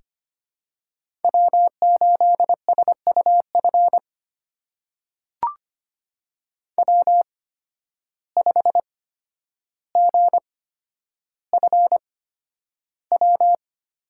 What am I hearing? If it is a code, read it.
Morse code, heard as a keyed tone, about 25 wpm, spaced as for 11 wpm, 700 Hz. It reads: W8SUF W5GFW
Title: USA Call Signs - 5 Characters - 8x Character Spacing - 25wpm